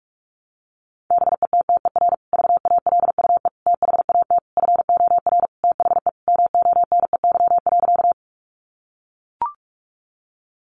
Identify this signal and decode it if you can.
Morse code, heard as a keyed tone, about 44 wpm, 700 Hz. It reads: BETTER VALUE THAT FOR THE MONEY.
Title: Sentences from Top 600 Words 45wpm